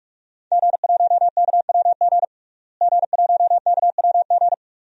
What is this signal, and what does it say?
Morse code, heard as a keyed tone, about 46 wpm, 700 Hz. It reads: G1KWG G1KWG
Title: DXCC-100 Call Signs - 5 Characters - Repeated 2x - 45wpm